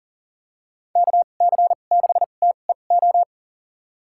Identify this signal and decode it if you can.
Morse code, heard as a keyed tone, about 40 wpm, spaced as for 29 wpm, 700 Hz. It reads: KC6TEO
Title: USA Call Signs - 6 Characters - 2x Character Spacing - 40wpm